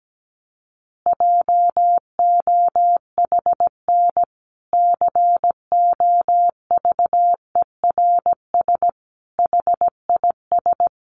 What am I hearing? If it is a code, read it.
Morse code, heard as a keyed tone, about 17 wpm, 700 Hz. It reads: JOHN COVERS HIS
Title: Easy Phrases 17wpm